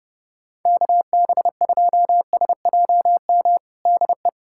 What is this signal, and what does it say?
Morse code, heard as a keyed tone, about 30 wpm, 700 Hz. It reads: KB2SJM DE
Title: USA Callsign DE Callsign - 6 Characters - 30wpm